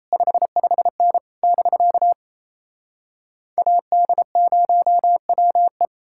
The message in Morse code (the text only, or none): 55N <BK> AD0WE